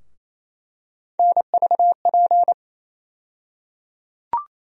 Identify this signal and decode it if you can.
Morse code, heard as a keyed tone, about 28 wpm, 700 Hz. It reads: NVP
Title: USA Call Sign Postfixes - 3 Letters 28wpm